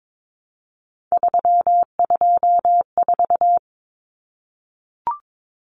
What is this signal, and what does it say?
Morse code, heard as a keyed tone, about 22 wpm, 700 Hz. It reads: 324
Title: Numbers - 3-Digit 22wpm